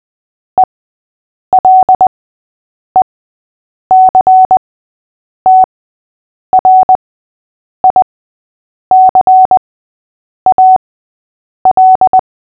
Morse code, heard as a keyed tone, about 20 wpm, spaced as for 8 wpm, 750 Hz. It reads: ELECTRICAL
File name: cw / fists / Short-Story-6-8wpm